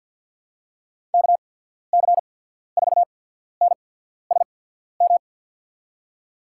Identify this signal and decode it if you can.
Morse code, heard as a keyed tone, about 53 wpm, spaced as for 14 wpm, 700 Hz. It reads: KC4NSM
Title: USA Call Signs - 6 Characters - 8x Character Spacing - 50wpm